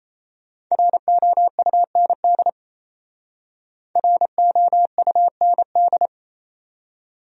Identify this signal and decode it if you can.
Morse code, heard as a keyed tone, about 35 wpm, 700 Hz. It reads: ROUND ROUND
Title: Speed-Racing Top 500 Words 28wpm